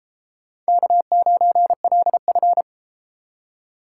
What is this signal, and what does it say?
Morse code, heard as a keyed tone, about 29 wpm, 700 Hz. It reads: K9LF